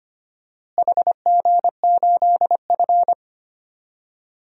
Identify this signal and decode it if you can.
Morse code, heard as a keyed tone, about 25 wpm, 700 Hz. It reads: HG8F
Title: DXCC-100 Call Signs - 4 Characters - Repeated 2x - 25wpm